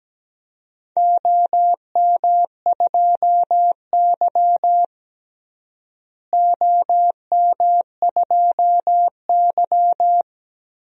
Morse code, heard as a keyed tone, about 17 wpm, 700 Hz. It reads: OM2Y OM2Y
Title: DXCC-100 Call Signs - 4 Characters - Repeated 3x - 17wpm